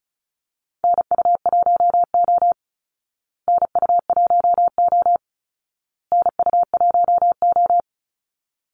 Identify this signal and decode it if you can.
Morse code, heard as a keyed tone, about 34 wpm, 700 Hz. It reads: NU1O NU1O NU1O
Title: Call Signs Repeated 2x - 35wpm